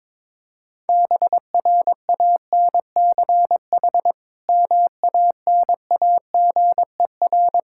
Morse code, heard as a keyed tone, about 22 wpm, 700 Hz. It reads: BRANCH MANAGER
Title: K1USN SST QSO Practice 22wpm